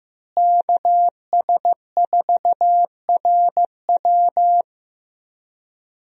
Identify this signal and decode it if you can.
Morse code, heard as a keyed tone, about 15 wpm, 700 Hz. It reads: KS4RW
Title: USA Call Signs - 5 Characters - Repeated 3x - 15wpm